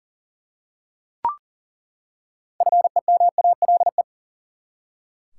Morse code, heard as a keyed tone, about 40 wpm, 700 Hz. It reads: FEMALE
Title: Six Letter Words - 40wpm